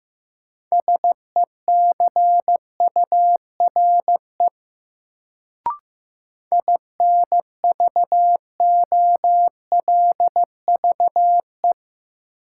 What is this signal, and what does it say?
Morse code, heard as a keyed tone, about 15 wpm, 700 Hz. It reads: SECURE INVOLVE
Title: Sentences from Top 1200 Words - New Word Drill Alt - 15wpm